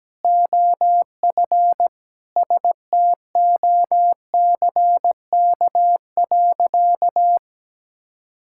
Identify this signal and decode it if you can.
Morse code, heard as a keyed tone, about 17 wpm, 700 Hz. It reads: OF STOCK.
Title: Sentences from Top 1000 Words - Review Part 5 - 17wpm